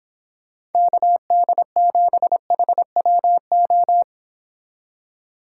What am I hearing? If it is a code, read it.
Morse code, heard as a keyed tone, about 31 wpm, 700 Hz. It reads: KD7HWO